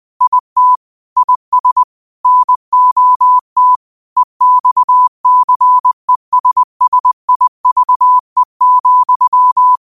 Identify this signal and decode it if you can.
Morse code, heard as a keyed tone, about 20 wpm, 1000 Hz. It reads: IT IS NOT EXCESSIVE,
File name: qotdfiles / 20wpm